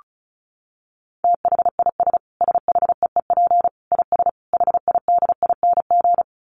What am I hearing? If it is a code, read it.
Morse code, heard as a keyed tone, about 34 wpm, 700 Hz. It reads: THIS SHEEP IS HIDING
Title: Dit Challenge 35wpm